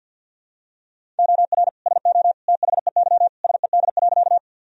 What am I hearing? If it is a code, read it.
Morse code, heard as a keyed tone, about 52 wpm, 700 Hz. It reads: OR SO THEY SED.
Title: Sentences from Top 100 Words and Abbreviations 50wpm